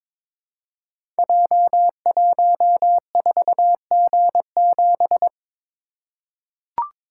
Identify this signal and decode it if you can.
Morse code, heard as a keyed tone, about 22 wpm, 700 Hz. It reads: J14G7
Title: Code Groups - 5 Letter:Numbers - 22wpm